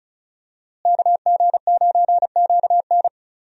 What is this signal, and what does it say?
Morse code, heard as a keyed tone, about 34 wpm, 700 Hz. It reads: KG9QN
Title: USA Call Signs - 5 Characters - Repeated 2x - 35wpm